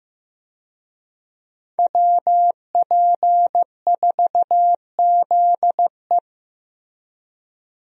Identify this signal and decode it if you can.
Morse code, heard as a keyed tone, about 15 wpm, 700 Hz. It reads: WP4ZE
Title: USA Call Signs - 5 Characters - Repeated 3x - 15wpm